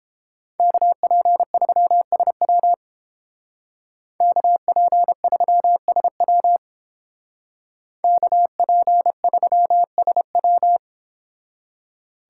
Morse code, heard as a keyed tone, about 29 wpm, 700 Hz. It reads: KP3SW KP3SW KP3SW